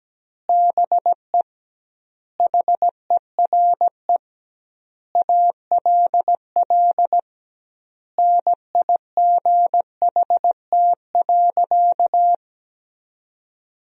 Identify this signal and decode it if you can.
Morse code, heard as a keyed tone, about 17 wpm, 700 Hz. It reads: BE HERE ALL NIGHT.
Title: 2x Word-Spacing - Sentences from Top 300 Words - 17wpm